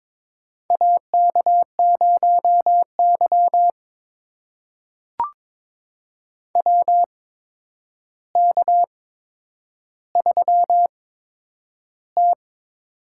Morse code, heard as a keyed tone, about 22 wpm, spaced as for 9 wpm, 700 Hz. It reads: AK0Y WK3T
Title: USA Call Signs - 4 Characters - 8x Character Spacing - 22wpm